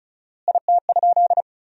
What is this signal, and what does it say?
Morse code, heard as a keyed tone, about 34 wpm, 700 Hz. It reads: IT?
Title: 4x Word-Spacing - Sentences from Top 300 Words - 35wpm